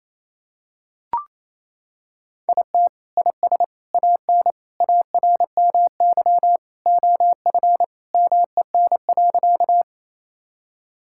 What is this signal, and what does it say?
Morse code, heard as a keyed tone, about 28 wpm, 700 Hz. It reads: IT IS AN ARMY OF MEN.
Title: Sentences from Top 1000 Words - Review Part 5 - 28wpm